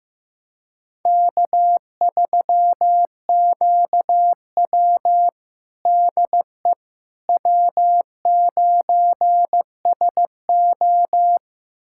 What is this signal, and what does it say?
Morse code, heard as a keyed tone, about 15 wpm, 700 Hz. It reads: K3QW DE W9SO
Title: USA Callsign DE Callsign - 4 Characters - 15wpm